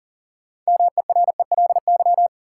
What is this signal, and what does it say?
Morse code, heard as a keyed tone, about 40 wpm, 700 Hz. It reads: MERELY
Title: Sentences from Top 1400 Words - New Word Drill - 40wpm